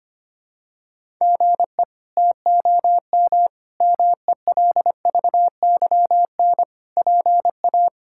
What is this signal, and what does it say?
Morse code, heard as a keyed tone, about 25 wpm, 700 Hz. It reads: GE TOM MELVYN PA
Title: K1USN SST QSO Practice - 25wpm